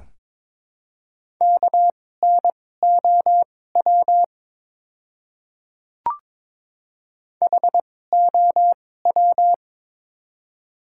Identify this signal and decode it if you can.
Morse code, heard as a keyed tone, about 21 wpm, spaced as for 16 wpm, 700 Hz. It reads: KNOW HOW
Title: 2x Character Spacing - Top 100 Words - 22wpm